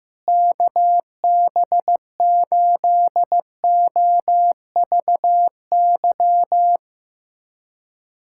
Morse code, heard as a keyed tone, about 15 wpm, 700 Hz. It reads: KB8OVY
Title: USA Callsign DE Callsign - 6 Characters - 15wpm